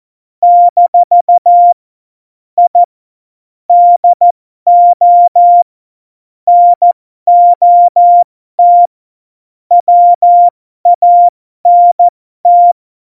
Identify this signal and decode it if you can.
Morse code, heard as a keyed tone, about 14 wpm, spaced as for 12 wpm, 700 Hz. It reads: - I DO NOT WANT
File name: Morsekurs / Witze